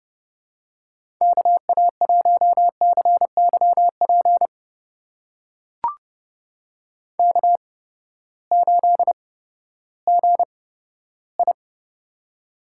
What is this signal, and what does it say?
Morse code, heard as a keyed tone, about 30 wpm, spaced as for 15 wpm, 700 Hz. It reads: KA1CYP K8GI
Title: USA Call Signs - 8x Character Spacing - 30wpm